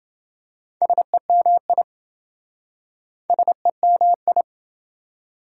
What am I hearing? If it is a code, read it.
Morse code, heard as a keyed tone, about 27 wpm, 700 Hz. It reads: SEMI SEMI